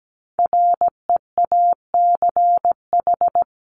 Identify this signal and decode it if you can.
Morse code, heard as a keyed tone, about 17 wpm, 700 Hz. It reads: REACH
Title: Top 500 Words Repeated 17wpm